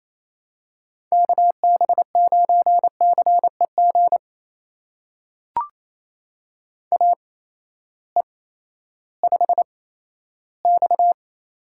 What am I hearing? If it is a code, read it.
Morse code, heard as a keyed tone, about 28 wpm, spaced as for 14 wpm, 700 Hz. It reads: KB9CEG AE5X
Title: USA Call Signs - 8x Character Spacing - 28wpm